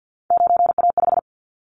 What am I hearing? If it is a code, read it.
Morse code, heard as a keyed tone, about 50 wpm, 700 Hz. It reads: 9A5
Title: Callsign Prefixes for Top 25 DXCC - 50wpm